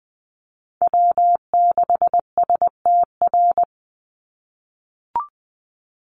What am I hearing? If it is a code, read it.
Morse code, heard as a keyed tone, about 20 wpm, 700 Hz. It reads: W6STR
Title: NA SOTA Chaser Callsigns 20wpm